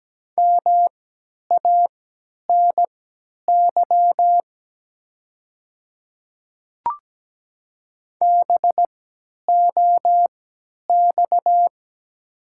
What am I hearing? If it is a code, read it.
Morse code, heard as a keyed tone, about 17 wpm, spaced as for 10 wpm, 700 Hz. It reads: MANY BOX
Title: 3x Character Spacing - Top 500 Words - 17wpm